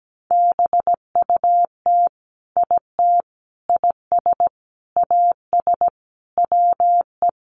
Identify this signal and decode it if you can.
Morse code, heard as a keyed tone, about 17 wpm, 700 Hz. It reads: BUT IT IS AS WE
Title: Sentences from Top 1500 Words 17wpm